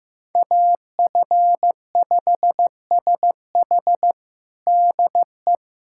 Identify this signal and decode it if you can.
Morse code, heard as a keyed tone, about 15 wpm, 700 Hz. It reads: AF5SH DE
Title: USA Callsign DE Callsign 15wpm